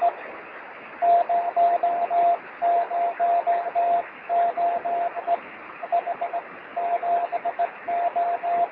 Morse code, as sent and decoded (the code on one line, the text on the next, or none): none